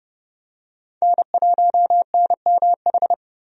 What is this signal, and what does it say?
Morse code, heard as a keyed tone, about 31 wpm, 700 Hz. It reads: N1NMH
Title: USA Call Signs - 5 Characters - 30wpm